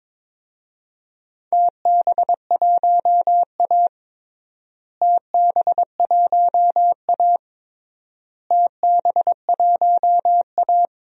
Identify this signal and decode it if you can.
Morse code, heard as a keyed tone, about 22 wpm, 700 Hz. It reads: TB1A TB1A TB1A
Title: DXCC-100 Call Signs - 4 Characters - Repeated 3x - 22wpm